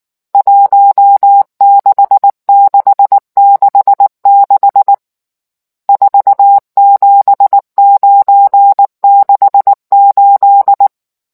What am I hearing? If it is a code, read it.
Morse code, heard as a keyed tone, about 19 wpm, 800 Hz. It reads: 16666 47968